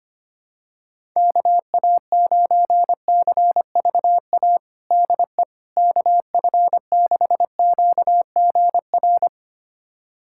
Morse code, heard as a keyed tone, about 25 wpm, 700 Hz. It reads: KA9CVA DE KF6QGR
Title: USA Callsign DE Callsign 25wpm